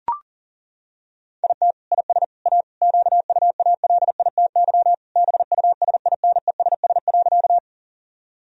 Morse code, heard as a keyed tone, about 37 wpm, 700 Hz. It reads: IT IS A QUALITY BUSINESS.
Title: Sentences from Top 1000 Words - Review Part 4 - 40wpm